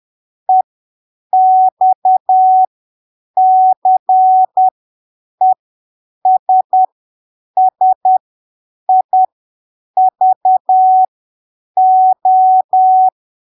Morse code, heard as a keyed tone, about 10 wpm, spaced as for 7 wpm, 750 Hz. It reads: EXCESSIVO
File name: arquivos / texto / texto 05 10ppm